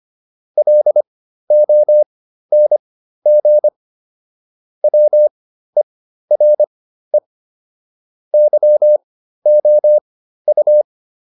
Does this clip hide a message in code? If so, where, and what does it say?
Yes: Morse code from 0.6 to 10.8 s, heard as a keyed tone, about 25 wpm, spaced as for 13 wpm, 600 Hz. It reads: LONG WERE YOU